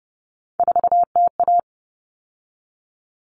Morse code, heard as a keyed tone, about 30 wpm, 700 Hz. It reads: 4TA
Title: All Cut Numbers - 3-Digit 30wpm